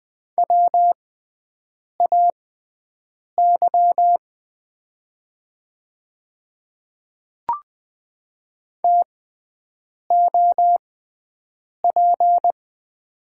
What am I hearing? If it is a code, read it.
Morse code, heard as a keyed tone, about 20 wpm, spaced as for 7 wpm, 700 Hz. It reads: WAY TOP
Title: 6x Character Spacing - Top 500 Words - 20wpm